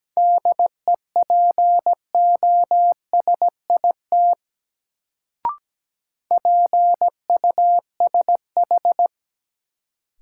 Morse code, heard as a keyed tone, about 17 wpm, 700 Hz. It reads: DEPOSIT PUSH
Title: Sentences from Top 1300 Words - New Word Drill Alt - 17wpm